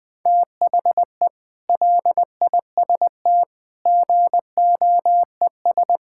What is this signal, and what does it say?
Morse code, heard as a keyed tone, about 21 wpm, 700 Hz. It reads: THE LIST GOES
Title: Sentences with 6-Word Phrases - Spoken Beginning - 20wpm